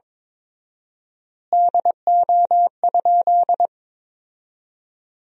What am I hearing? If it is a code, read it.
Morse code, heard as a keyed tone, about 22 wpm, 700 Hz. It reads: DO?
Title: 4x Word-Spacing - Sentences from Top 100 Words - 22wpm